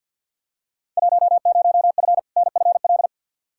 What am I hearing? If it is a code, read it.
Morse code, heard as a keyed tone, about 46 wpm, 700 Hz. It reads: 10F NFL
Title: Field Day Exchanges Encore - 50wpm